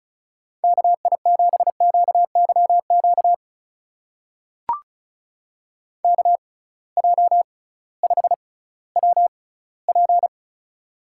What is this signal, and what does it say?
Morse code, heard as a keyed tone, about 36 wpm, spaced as for 19 wpm, 700 Hz. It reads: KI7QYQ KJ5WP
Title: USA Call Signs - 6x Character Spacing - 35wpm